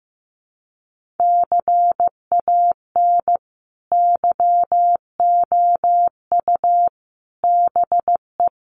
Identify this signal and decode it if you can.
Morse code, heard as a keyed tone, about 14 wpm, 700 Hz. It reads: CAN YOU BE
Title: Sentences from Top 1100 Words 15wpm